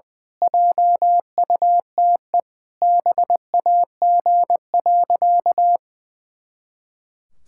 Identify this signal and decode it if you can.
Morse code, heard as a keyed tone, about 20 wpm, 700 Hz. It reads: JUTE BAG.
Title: Pangram Challenge v2 - 20wpm